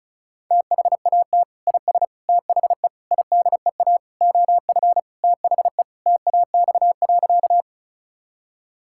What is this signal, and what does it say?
Morse code, heard as a keyed tone, about 36 wpm, 700 Hz. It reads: THAT IS THE IDEA OF THE TAX.
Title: Sentences from Top 1000 Words - Review Part 5 - 35wpm